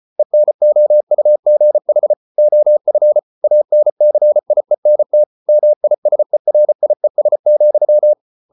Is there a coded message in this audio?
Yes: Morse code from 0.2 to 8.1 s, heard as a keyed tone, about 34 wpm, 600 Hz. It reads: ENOUGH OF ANCIENT MISERIES,